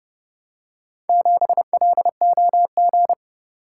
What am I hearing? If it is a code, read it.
Morse code, heard as a keyed tone, about 30 wpm, 700 Hz. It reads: 7LOG